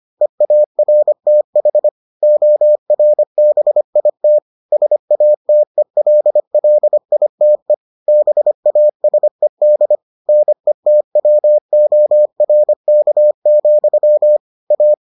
Morse code, heard as a keyed tone, about 25 wpm, 600 Hz. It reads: EARTH ORBIT SATELLITE BASED NETWORK, A